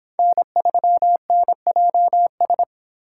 Morse code, heard as a keyed tone, about 31 wpm, 700 Hz. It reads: N3NJS